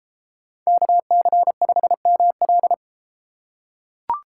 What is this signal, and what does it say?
Morse code, heard as a keyed tone, about 33 wpm, 690 Hz. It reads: KC5ML